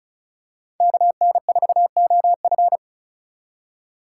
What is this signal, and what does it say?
Morse code, heard as a keyed tone, about 36 wpm, 700 Hz. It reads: KN4OF
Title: USA Call Signs - 3x Character Spacing - 35wpm